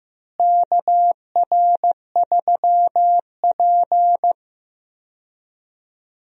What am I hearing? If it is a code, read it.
Morse code, heard as a keyed tone, about 15 wpm, 700 Hz. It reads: KR3P
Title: USA Call Signs - 4 Characters - Repeated 2x - 15wpm